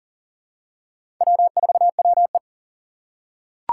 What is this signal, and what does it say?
Morse code, heard as a keyed tone, about 37 wpm, 700 Hz. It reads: W4WE
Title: Four Challenging Letter-Number Combinations 40wpm